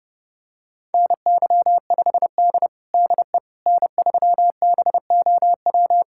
Morse code, heard as a keyed tone, about 30 wpm, 700 Hz. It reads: NY5D DE N3BOW
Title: USA Callsign DE Callsign 30wpm